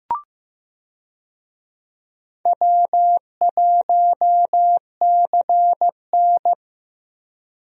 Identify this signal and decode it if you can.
Morse code, heard as a keyed tone, about 15 wpm, 700 Hz. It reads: W1CN
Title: USA Call Signs - 4 Characters - Repeated 3x - 15wpm